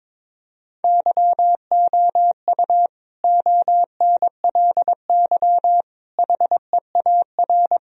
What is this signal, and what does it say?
Morse code, heard as a keyed tone, about 22 wpm, 700 Hz. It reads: YOU ONLY HEAR
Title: Sentences from Top 1000 Words - Review Part 2 - 22wpm